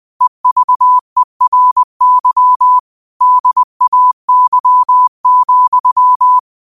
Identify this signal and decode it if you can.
Morse code, heard as a keyed tone, about 20 wpm, 1000 Hz. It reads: EVERY DAY,
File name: qotdfiles / 20wpm